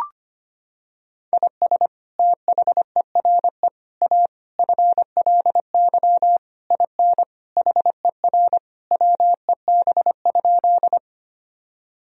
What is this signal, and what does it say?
Morse code, heard as a keyed tone, about 25 wpm, 700 Hz. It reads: IS THERE A FLY IN HER WEB?
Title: Sentences from Top 1000 Words - Review Part 4 - 25wpm